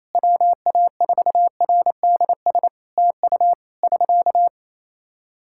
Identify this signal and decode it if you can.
Morse code, heard as a keyed tone, about 28 wpm, 700 Hz. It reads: WA4RDS TU <SK>
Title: Practice QSO - Exchange 4 - Ending - 28wpm